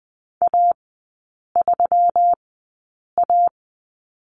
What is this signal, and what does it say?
Morse code, heard as a keyed tone, about 20 wpm, spaced as for 8 wpm, 700 Hz. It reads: A3A
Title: 178 Lesson 29e - Slash - Three Characters 20wpm